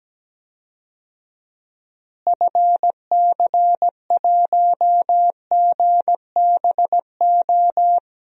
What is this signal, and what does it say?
Morse code, heard as a keyed tone, about 17 wpm, 700 Hz. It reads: FC1GBO
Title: DXCC-100 Call Signs - 6 Characters - Repeated 2x - 17wpm